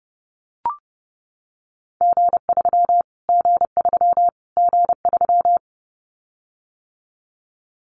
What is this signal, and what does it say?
Morse code, heard as a keyed tone, about 30 wpm, 700 Hz. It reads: G3 G3 G3
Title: Callsign Prefixes for Top 25 DXCC - Repeated 3x - 30wpm